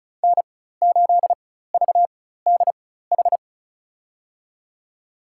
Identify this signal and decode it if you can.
Morse code, heard as a keyed tone, about 36 wpm, spaced as for 16 wpm, 700 Hz. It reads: N8VDH
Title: USA Call Signs - 5 Characters - 4x Character Spacing - 35wpm